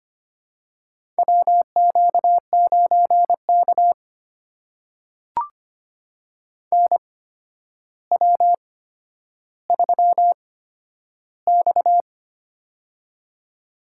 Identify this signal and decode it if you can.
Morse code, heard as a keyed tone, about 25 wpm, spaced as for 11 wpm, 700 Hz. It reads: WQ9K NW3X
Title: USA Call Signs - 4 Characters - 8x Character Spacing - 25wpm